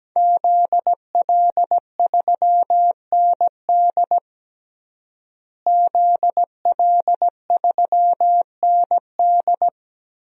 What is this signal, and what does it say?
Morse code, heard as a keyed tone, about 17 wpm, 700 Hz. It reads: ZL3ND ZL3ND
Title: DXCC-100 Call Signs - 5 Characters - Repeated 3x - 17wpm